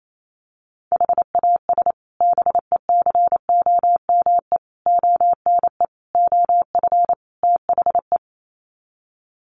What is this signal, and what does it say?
Morse code, heard as a keyed tone, about 28 wpm, 700 Hz. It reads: HAS BECOME ONE OF THE
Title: Sets of 5 Words Encore 28wpm